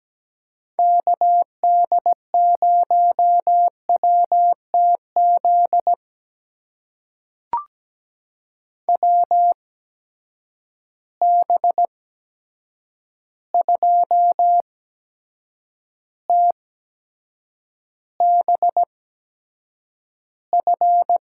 Morse code, heard as a keyed tone, about 17 wpm, spaced as for 7 wpm, 700 Hz. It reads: KD0WTZ WB2TBF
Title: USA Call Signs - 6 Characters - 8x Character Spacing - 17wpm